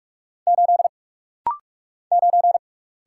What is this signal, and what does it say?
Morse code, heard as a keyed tone, about 44 wpm, 690 Hz. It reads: 8 9
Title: Single Letter-Number - Rapid-Fire 45wpm